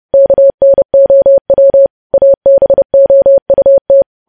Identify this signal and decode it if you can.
Morse code, heard as a keyed tone, about 30 wpm, 550 Hz. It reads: KNOW ABOUT
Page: news headlines in morse code at 30 wpm